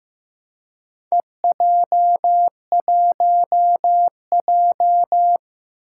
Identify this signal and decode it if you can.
Morse code, heard as a keyed tone, about 15 wpm, 700 Hz. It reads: EJ1J
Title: Four Challenging Letter-Number Combinations 15wpm